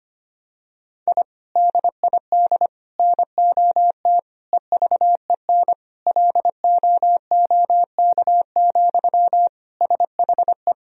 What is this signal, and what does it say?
Morse code, heard as a keyed tone, about 25 wpm, 700 Hz. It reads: I DID NOT EVEN LOOK, SHE
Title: Sentences from Top 1000 Words - Review Part 1 - 25wpm